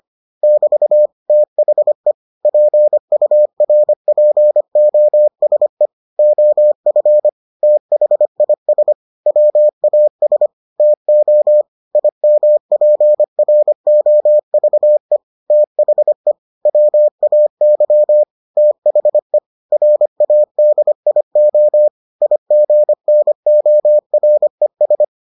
Morse code, heard as a keyed tone, about 25 wpm, 600 Hz. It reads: <BT> THE PURPOSE OF THIS WAS TO IMPROVE THE WAY THE RADIO IGNORES